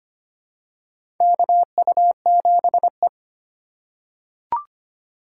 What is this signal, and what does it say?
Morse code, heard as a keyed tone, about 25 wpm, 700 Hz. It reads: KU7E
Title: USA Call Signs - 8x Character Spacing - 25wpm